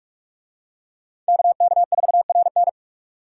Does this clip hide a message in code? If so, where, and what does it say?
Yes: Morse code from 1.3 to 2.7 s, heard as a keyed tone, about 41 wpm, 700 Hz. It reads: KK4RN